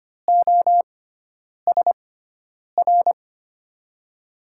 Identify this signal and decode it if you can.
Morse code, heard as a keyed tone, about 25 wpm, spaced as for 9 wpm, 700 Hz. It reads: OSR
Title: USA Call Signs - 6 Characters - 6x Character Spacing - 25wpm